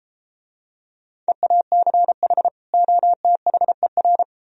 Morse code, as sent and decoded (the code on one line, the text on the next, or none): . .- -.-. .... / --- - .... .-.
EACH OTHER